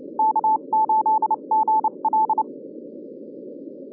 Morse code, heard as a keyed tone, about 30 wpm, spaced as for 26 wpm, 840 Hz. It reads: K8GL
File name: 2011 bcc-challenge sonntag tape